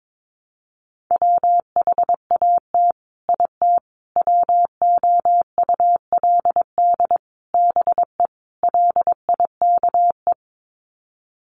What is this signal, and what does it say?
Morse code, heard as a keyed tone, about 22 wpm, 700 Hz. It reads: WHAT IT WOULD BE LIKE
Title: Sets of 5 Words Encore 22wpm